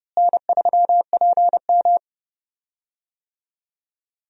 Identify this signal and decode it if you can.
Morse code, heard as a keyed tone, about 29 wpm, 700 Hz. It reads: N3PM
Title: USA Call Signs - 4 Characters - Repeated 2x - 30wpm